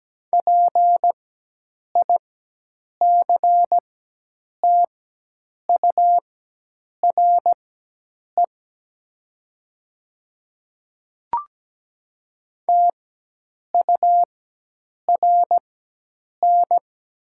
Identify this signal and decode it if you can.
Morse code, heard as a keyed tone, about 17 wpm, spaced as for 8 wpm, 700 Hz. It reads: PICTURE TURN
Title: 4x Character Spacing - Top 200 Words - 17wpm